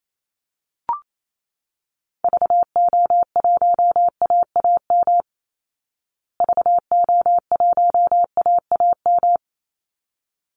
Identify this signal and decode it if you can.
Morse code, heard as a keyed tone, about 28 wpm, 700 Hz. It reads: VO1AAM VO1AAM
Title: Call Signs Repeated 3x - 28wpm